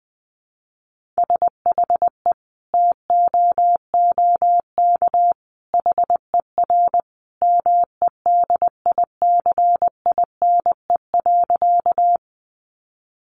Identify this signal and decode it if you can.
Morse code, heard as a keyed tone, about 20 wpm, 700 Hz. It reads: SHE TOOK HER MEDICINE.
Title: Sentences from Top 900 Words 20wpm